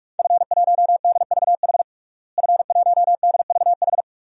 Morse code, heard as a keyed tone, about 42 wpm, 700 Hz. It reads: F1DVH F1DVH
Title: DXCC-100 Call Signs - 5 Characters - Repeated 2x - 45wpm